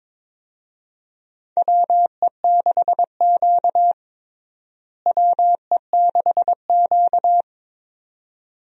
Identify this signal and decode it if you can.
Morse code, heard as a keyed tone, about 22 wpm, 700 Hz. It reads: WE6Q WE6Q
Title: USA Call Signs - 4 Characters - Repeated 2x - 22wpm